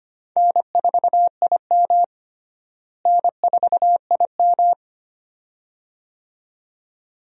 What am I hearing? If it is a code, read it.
Morse code, heard as a keyed tone, about 25 wpm, 700 Hz. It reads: N4IM N4IM
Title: USA Call Signs - 4 Characters - Repeated 2x - 25wpm